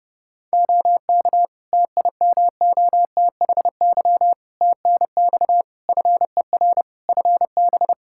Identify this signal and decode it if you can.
Morse code, heard as a keyed tone, about 30 wpm, 700 Hz. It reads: OK TIMOTHY TNX FER FB
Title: K1USN SST QSO Practice 30wpm